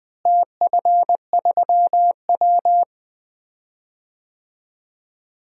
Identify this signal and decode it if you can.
Morse code, heard as a keyed tone, about 20 wpm, 700 Hz. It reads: TF3W
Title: DXCC-100 Call Signs - 4 Characters - Repeated 3x - 20wpm